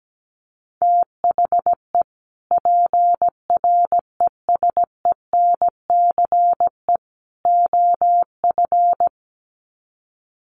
Morse code, heard as a keyed tone, about 17 wpm, 700 Hz. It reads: THE PRESENCE OF
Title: Sets of 3 Words 17wpm